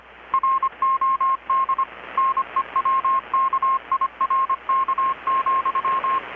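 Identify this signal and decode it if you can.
Morse code, heard as a keyed tone, about 25 wpm, 1070 Hz. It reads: ROD NEWKIRK,
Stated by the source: Morse from KPH's Night of Nights XX maritime radiotelegraphy event, received in Chiapas, Mexico about 3,800 km away on a Kenwood R-600 with a grounded 10 m long-wire antenna.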